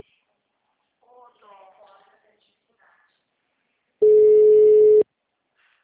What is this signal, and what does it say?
Ringback tone (Europe, 425 Hz), with speech.